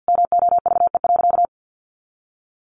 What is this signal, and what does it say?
Morse code, heard as a keyed tone, about 50 wpm, 700 Hz. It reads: MOVE.
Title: A Sound Of Thunder 50wpm